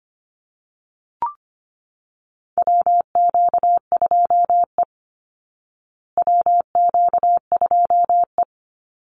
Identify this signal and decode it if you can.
Morse code, heard as a keyed tone, about 25 wpm, 700 Hz. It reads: WQ2E WQ2E
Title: Call Signs Repeated 3x - 25wpm